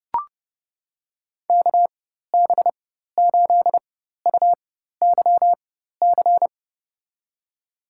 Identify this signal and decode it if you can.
Morse code, heard as a keyed tone, about 31 wpm, spaced as for 14 wpm, 700 Hz. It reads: KB8UYC
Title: USA Call Signs - 4x Character Spacing - 30wpm